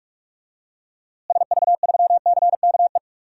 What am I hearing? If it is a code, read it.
Morse code, heard as a keyed tone, about 41 wpm, 700 Hz. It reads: SV3CKE
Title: DXCC-100 Call Signs - 6 Characters - Repeated 3x - 45wpm